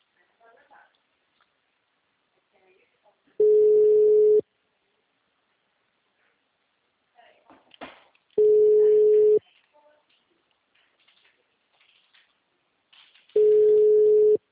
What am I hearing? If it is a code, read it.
Ringback tone (Europe, 425 Hz), with speech.